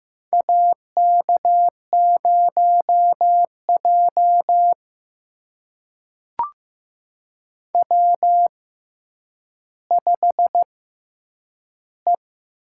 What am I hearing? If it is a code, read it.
Morse code, heard as a keyed tone, about 15 wpm, spaced as for 9 wpm, 700 Hz. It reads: AK0J W5E
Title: USA Call Signs - 4 Characters - 6x Character Spacing - 15wpm